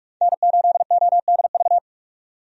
Morse code, heard as a keyed tone, about 45 wpm, 700 Hz. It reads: N8ODV
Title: USA Call Signs - 5 Characters - 45wpm